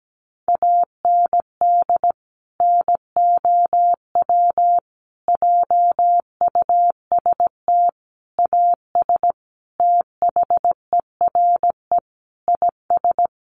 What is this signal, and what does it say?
Morse code, heard as a keyed tone, about 17 wpm, 700 Hz. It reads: AND NOW JUST AS THERE IS
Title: Sentences from Top 1200 Words 17wpm